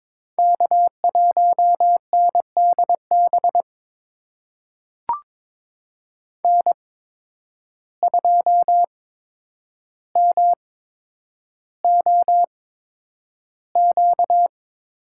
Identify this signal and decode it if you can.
Morse code, heard as a keyed tone, about 22 wpm, spaced as for 9 wpm, 700 Hz. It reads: K1NDB N2MOQ
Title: USA Call Signs - 5 Characters - 8x Character Spacing - 22wpm